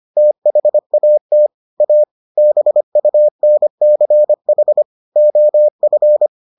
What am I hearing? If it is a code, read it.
Morse code, heard as a keyed tone, about 25 wpm, 600 Hz. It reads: THAT A BUNCH OF